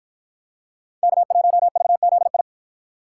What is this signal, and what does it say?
Morse code, heard as a keyed tone, about 53 wpm, 700 Hz. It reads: K1VZI